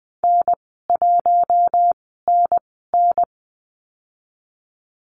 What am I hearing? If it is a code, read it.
Morse code, heard as a keyed tone, about 20 wpm, spaced as for 15 wpm, 700 Hz. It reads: N1NN
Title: 048 Lesson 9g - Number 4 - Call Signs 20wpm